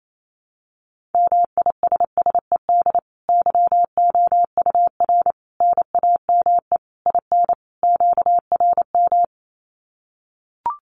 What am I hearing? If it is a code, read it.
Morse code, heard as a keyed tone, about 28 wpm, 700 Hz. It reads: MISSED YOUR NAME IN QRM
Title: Common QSO Phrases 28wpm